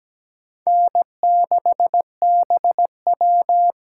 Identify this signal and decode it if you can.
Morse code, heard as a keyed tone, about 17 wpm, 700 Hz. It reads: N6BW
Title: USA Call Signs - 4 Characters - 3x Character Spacing - 17wpm